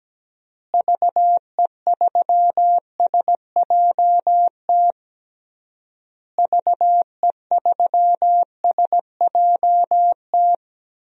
Morse code, heard as a keyed tone, about 17 wpm, 700 Hz. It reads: VE3SJT VE3SJT
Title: DXCC-100 Call Signs - 6 Characters - Repeated 3x - 17wpm